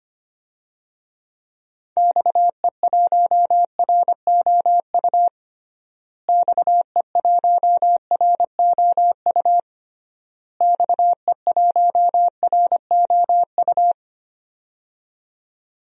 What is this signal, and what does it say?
Morse code, heard as a keyed tone, about 25 wpm, 700 Hz. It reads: XE1ROU XE1ROU XE1ROU
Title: DXCC-100 Call Signs - 6 Characters - Repeated 3x - 25wpm